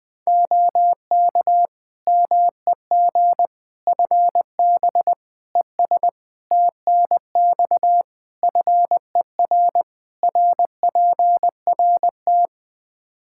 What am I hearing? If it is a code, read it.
Morse code, heard as a keyed tone, about 20 wpm, 700 Hz. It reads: OK MEG FB ES TNX FER RPRT
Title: Practice QSO - Exchange 2 - Info - 20wpm